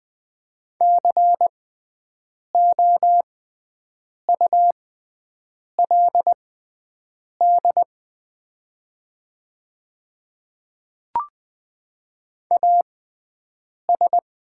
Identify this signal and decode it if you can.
Morse code, heard as a keyed tone, about 20 wpm, spaced as for 7 wpm, 690 Hz. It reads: COULD AS
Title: 6x Character Spacing - Top 100 Words - 20wpm